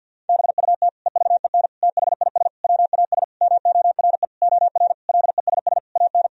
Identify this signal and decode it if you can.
Morse code, heard as a keyed tone, about 50 wpm, 700 Hz. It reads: BUT EVEN THIS WAS MORE OR LESS AN
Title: Sentences from Top 1000 Words - Review Part 2 - 50wpm